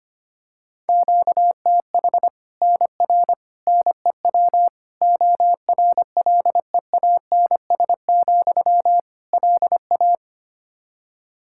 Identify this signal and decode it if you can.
Morse code, heard as a keyed tone, about 25 wpm, 700 Hz. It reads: QTH NR NEW ORLEANS, LA
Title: K1USN SST QSO Practice 25wpm